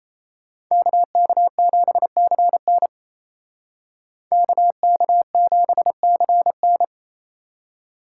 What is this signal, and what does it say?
Morse code, heard as a keyed tone, about 36 wpm, 700 Hz. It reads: KK7CN KK7CN